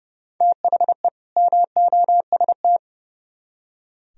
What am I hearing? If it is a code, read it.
Morse code, heard as a keyed tone, about 36 wpm, 700 Hz. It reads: THE MOST
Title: Speed-Racing - Sets of 2 Words 30wpm